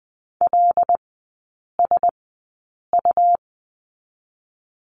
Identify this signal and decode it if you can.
Morse code, heard as a keyed tone, about 20 wpm, spaced as for 8 wpm, 700 Hz. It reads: LSU
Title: 114 Lesson 19e - Letter w - Three Characters 20wpm